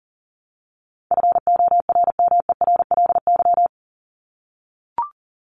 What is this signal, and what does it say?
Morse code, heard as a keyed tone, about 40 wpm, 700 Hz. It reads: FORMERLY
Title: LY Suffix - Top 100 Words with Base Words 40wpm